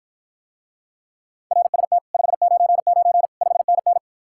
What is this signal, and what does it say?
Morse code, heard as a keyed tone, about 54 wpm, 700 Hz. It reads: RST 599 5NN